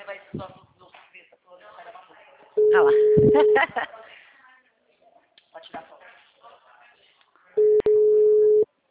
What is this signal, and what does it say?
Ringback tone (Europe, 425 Hz), with speech.